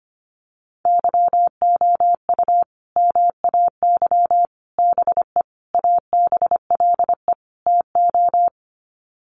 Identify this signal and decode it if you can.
Morse code, heard as a keyed tone, about 25 wpm, 700 Hz. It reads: YOU MAY BE ABLE TO
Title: Sets of 5 Words Encore 25wpm